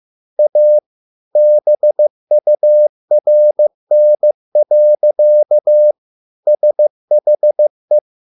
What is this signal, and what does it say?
Morse code, heard as a keyed tone, about 15 wpm, 600 Hz. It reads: A BURN. SHE